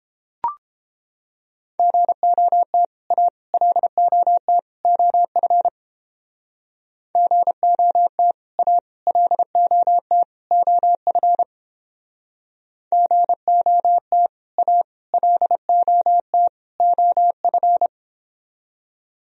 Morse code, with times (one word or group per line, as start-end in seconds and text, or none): 1.8-2.8 GOT
3.1-3.3 A
3.5-4.6 LOT
4.8-5.7 OF
7.2-8.3 GOT
8.6-8.8 A
9.1-10.2 LOT
10.5-11.4 OF
12.9-14.3 GOT
14.6-14.8 A
15.1-16.5 LOT
16.8-17.9 OF